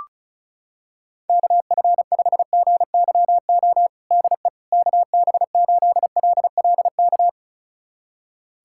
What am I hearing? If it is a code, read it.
Morse code, heard as a keyed tone, about 36 wpm, 700 Hz. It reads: KF5GYO DE KB8LLK
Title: USA Callsign DE Callsign 35wpm